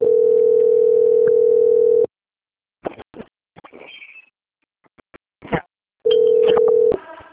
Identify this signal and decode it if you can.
Ringback tone (North America), 440 + 480 Hz.
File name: out- - - - - .58237